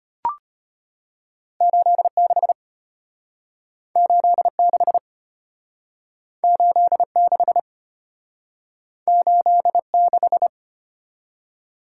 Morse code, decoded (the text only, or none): 86 86 86 86